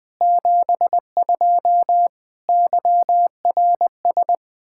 Morse code, heard as a keyed tone, about 20 wpm, 700 Hz. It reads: 72 YRS
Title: K1USN SST QSO Practice 20wpm